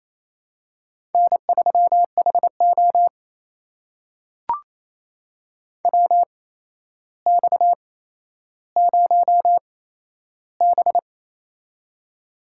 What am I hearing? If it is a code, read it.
Morse code, heard as a keyed tone, about 28 wpm, spaced as for 12 wpm, 700 Hz. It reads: N3HO WX0B
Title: USA Call Signs - 4 Characters - 8x Character Spacing - 28wpm